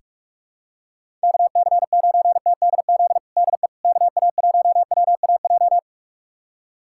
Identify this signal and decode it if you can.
Morse code, heard as a keyed tone, about 46 wpm, 700 Hz. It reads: KC9TDZ DE KA1WAJ
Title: USA Callsign DE Callsign - 6 Characters - 45wpm